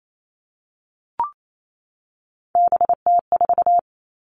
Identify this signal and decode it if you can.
Morse code, heard as a keyed tone, about 28 wpm, 700 Hz. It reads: BT4
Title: All Cut Numbers - 3-Digit 28wpm